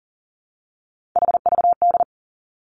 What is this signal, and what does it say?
Morse code, heard as a keyed tone, about 40 wpm, 700 Hz. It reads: HVD
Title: Three Challenging Letter-Number Combinations 40wpm